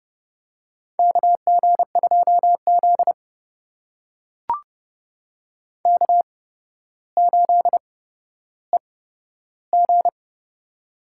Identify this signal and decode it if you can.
Morse code, heard as a keyed tone, about 32 wpm, spaced as for 13 wpm, 700 Hz. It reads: KG2Z K8EG
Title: USA Call Signs - 4 Characters - 8x Character Spacing - 30wpm